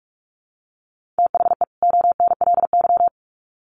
Morse code, heard as a keyed tone, about 44 wpm, 700 Hz. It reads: THE ONLY
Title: Sets of 2 Words 45wpm